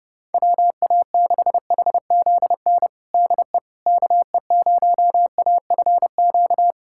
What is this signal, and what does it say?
Morse code, heard as a keyed tone, about 30 wpm, 700 Hz. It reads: WA6HZN DE KE0AFQ
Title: USA Callsign DE Callsign - 6 Characters - 30wpm